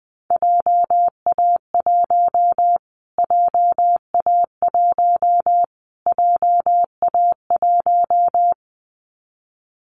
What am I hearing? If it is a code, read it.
Morse code, heard as a keyed tone, about 20 wpm, 700 Hz. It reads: JA1 JA1 JA1
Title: Callsign Prefixes for Top 25 DXCC - Repeated 3x - 20wpm